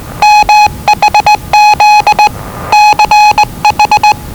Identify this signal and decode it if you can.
Morse code, heard as a keyed tone, about 20 wpm, 840 Hz. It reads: MHZ CH
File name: WellingtonRadioZlwCloseDownOnHfAndMf / 01-Zlw 4277khz 417.5 qrt until 0925z